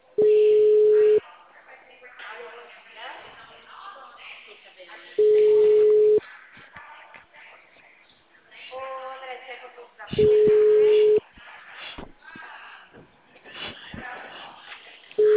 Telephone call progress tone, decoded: ringback tone

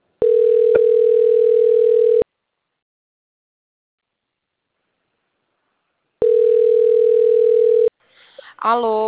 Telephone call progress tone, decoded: ringback tone

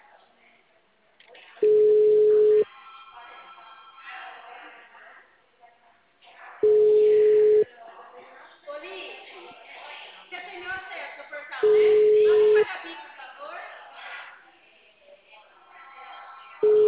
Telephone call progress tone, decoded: ringback tone

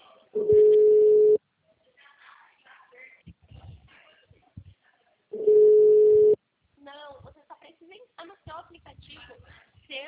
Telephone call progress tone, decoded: ringback tone

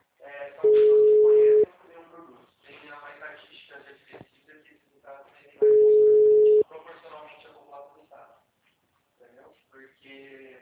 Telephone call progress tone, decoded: ringback tone

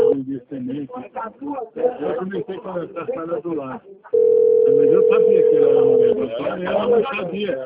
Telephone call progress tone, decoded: ringback tone

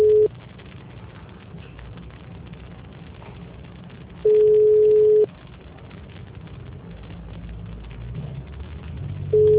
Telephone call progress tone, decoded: ringback tone

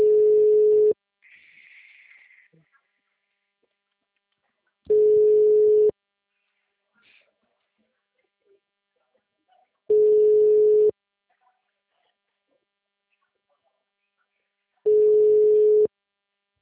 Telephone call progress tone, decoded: ringback tone

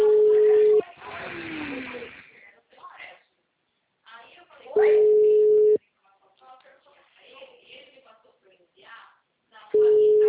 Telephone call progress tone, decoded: ringback tone

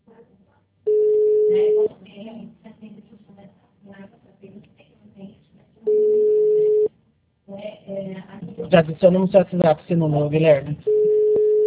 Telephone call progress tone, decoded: ringback tone

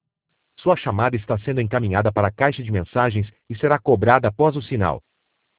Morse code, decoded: none

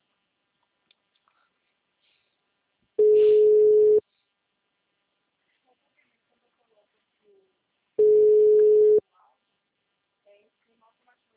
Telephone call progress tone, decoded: ringback tone

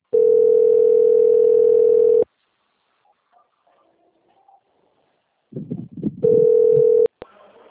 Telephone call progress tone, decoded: ringback tone

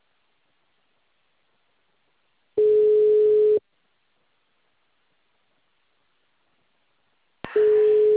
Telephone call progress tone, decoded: ringback tone

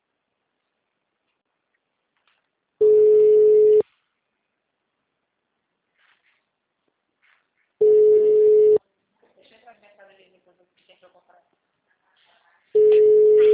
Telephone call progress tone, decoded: ringback tone

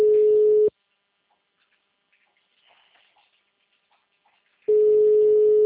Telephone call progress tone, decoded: ringback tone

